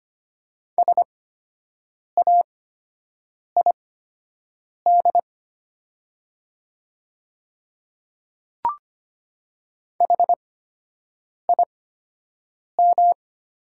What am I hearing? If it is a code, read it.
Morse code, heard as a keyed tone, about 25 wpm, spaced as for 7 wpm, 700 Hz. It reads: SAID HIM
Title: 8x Character Spacing - Top 500 Words - 25wpm